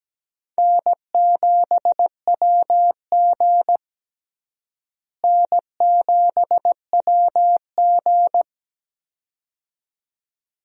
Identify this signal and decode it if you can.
Morse code, heard as a keyed tone, about 17 wpm, 700 Hz. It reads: N7WG N7WG
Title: USA Call Signs - 4 Characters - Repeated 2x - 17wpm